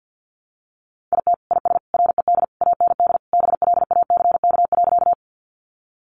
Morse code, heard as a keyed tone, about 48 wpm, 700 Hz. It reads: IT IS RED AND BLACK.
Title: Easy Phrases 50wpm